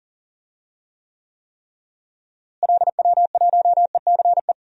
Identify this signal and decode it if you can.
Morse code, heard as a keyed tone, about 38 wpm, 700 Hz. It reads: LW1ECE
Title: DXCC-100 Call Signs - 6 Characters - Repeated 3x - 40wpm